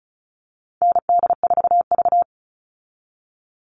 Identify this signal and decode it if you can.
Morse code, heard as a keyed tone, about 34 wpm, 700 Hz. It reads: ND4V
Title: Call Signs Repeated 2x - 35wpm